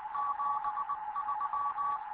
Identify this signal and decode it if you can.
Morse code, heard as a keyed tone, about 20 wpm, 1140 Hz, with noise. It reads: 73
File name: Morze-HA2MN / hangp / 2 11 GYAKORLAT